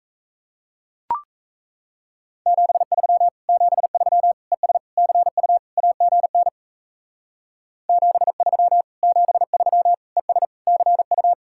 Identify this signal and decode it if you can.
Morse code, heard as a keyed tone, about 37 wpm, 700 Hz. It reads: 73 73 ES CU AGN 73 73 ES CU